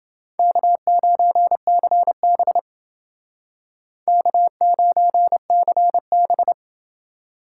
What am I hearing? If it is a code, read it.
Morse code, heard as a keyed tone, about 29 wpm, 700 Hz. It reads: K9CB K9CB